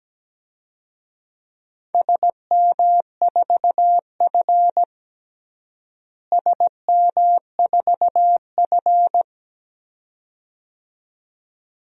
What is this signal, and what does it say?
Morse code, heard as a keyed tone, about 17 wpm, 700 Hz. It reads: SM4F SM4F
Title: DXCC-100 Call Signs - 4 Characters - Repeated 2x - 17wpm